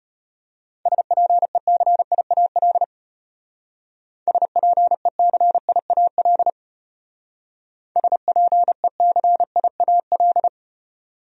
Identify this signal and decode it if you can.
Morse code, heard as a keyed tone, about 34 wpm, 700 Hz. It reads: SPECIAL SPECIAL SPECIAL